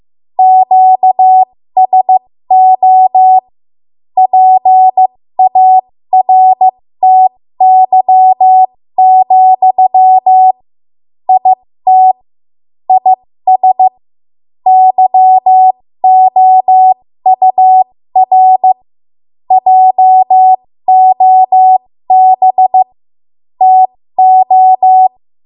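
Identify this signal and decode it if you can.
Morse code, heard as a keyed tone, about 15 wpm, spaced as for 13 wpm, 750 Hz. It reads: QSO PARTY, IT IS YOUR JOB TO
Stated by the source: ARRL Morse code practice at 13 WPM.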